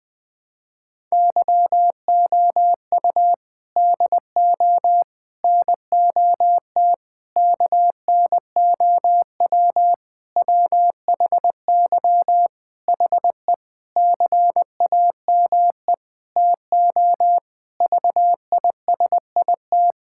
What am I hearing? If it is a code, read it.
Morse code, heard as a keyed tone, about 20 wpm, 700 Hz. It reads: YOU DO NOT KNOW WHY HE CAME TO VISIT
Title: Sentences from Top 1000 Words - Review Part 6 - 20wpm